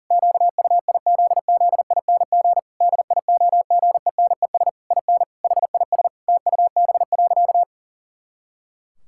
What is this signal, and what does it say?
Morse code, heard as a keyed tone, about 40 wpm, 700 Hz. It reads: QUIZZING DIOGENES IN HIS TUB.
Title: Pangram Challenge v2 - 40wpm